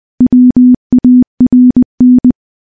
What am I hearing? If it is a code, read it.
Morse code, heard as a keyed tone, about 21 wpm, 250 Hz. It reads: WARN
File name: morse / morse-training-20WPM